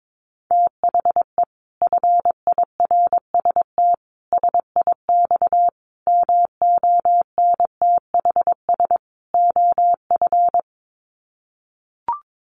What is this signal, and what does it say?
Morse code, heard as a keyed tone, about 22 wpm, 700 Hz. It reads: THE FIRST SIX MONTHS OF
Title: Sets of 5 Words Encore2 22wpm